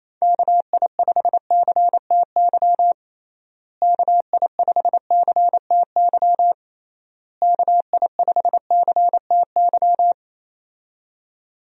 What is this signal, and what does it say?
Morse code, heard as a keyed tone, about 28 wpm, 700 Hz. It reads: KI5CTY KI5CTY KI5CTY
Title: USA Call Signs - 6 Characters - Repeated 3x - 28wpm